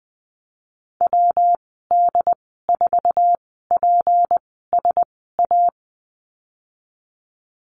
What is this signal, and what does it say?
Morse code, heard as a keyed tone, about 20 wpm, spaced as for 15 wpm, 700 Hz. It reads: WD4PSA
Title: 154 Lesson 25g - Letter p - Call Signs 20wpm